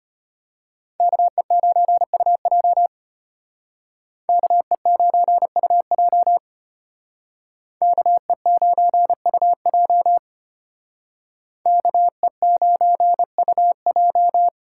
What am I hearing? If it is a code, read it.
Morse code, heard as a keyed tone, about 32 wpm, 700 Hz. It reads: KE9UJ KE9UJ KE9UJ KE9UJ